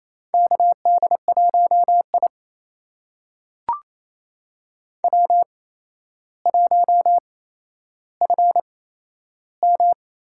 Morse code, heard as a keyed tone, about 28 wpm, spaced as for 12 wpm, 700 Hz. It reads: KD1I W1FM
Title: USA Call Signs - 4 Characters - 8x Character Spacing - 28wpm